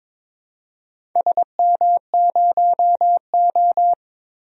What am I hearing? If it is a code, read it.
Morse code, heard as a keyed tone, about 22 wpm, 700 Hz. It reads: SM0O
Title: DXCC-100 Call Signs - 4 Characters - 22wpm